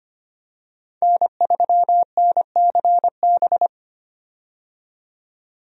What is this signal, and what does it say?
Morse code, heard as a keyed tone, about 25 wpm, 700 Hz. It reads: N3NCB
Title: USA Call Signs - 5 Characters - 25wpm